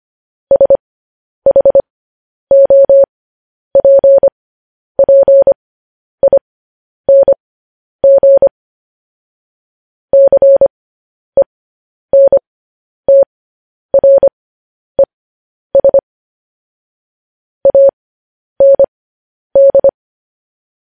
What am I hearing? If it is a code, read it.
Morse code, heard as a keyed tone, about 25 wpm, spaced as for 10 wpm, 550 Hz. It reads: SHOPPING CENTRES AND
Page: news-headlines-in-morse-code-at-10-wpm